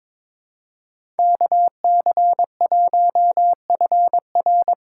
Morse code, heard as a keyed tone, about 21 wpm, 700 Hz. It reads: KC1FR
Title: USA Call Signs - 6 Characters - 6x Character Spacing - 22wpm